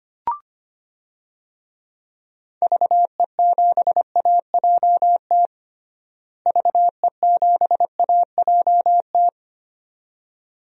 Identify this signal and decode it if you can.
Morse code, heard as a keyed tone, about 25 wpm, 700 Hz. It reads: VE7AJT VE7AJT
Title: DXCC-100 Call Signs - 6 Characters - Repeated 2x - 25wpm